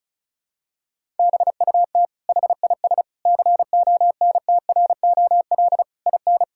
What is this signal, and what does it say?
Morse code, heard as a keyed tone, about 36 wpm, 700 Hz. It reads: BUT HIS CONTROL IN
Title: Sentences from Top 1000 Words - Review Part 6 - 35wpm